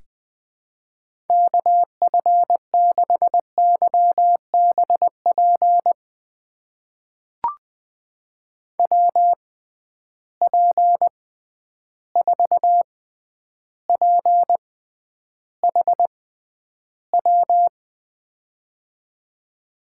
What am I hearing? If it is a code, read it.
Morse code, heard as a keyed tone, about 20 wpm, spaced as for 10 wpm, 700 Hz. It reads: KF6YBP WP4PHW
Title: USA Call Signs - 6 Characters - 6x Character Spacing - 20wpm